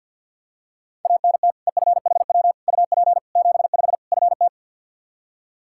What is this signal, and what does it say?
Morse code, heard as a keyed tone, about 53 wpm, 700 Hz. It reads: ANT EFHW UP 75 FT
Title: Practice QSO - Exchange 2 - Info - 50wpm